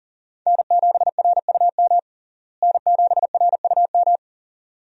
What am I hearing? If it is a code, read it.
Morse code, heard as a keyed tone, about 39 wpm, 700 Hz. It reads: N7RUM N7RUM
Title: USA Call Signs - 5 Characters - Repeated 2x - 40wpm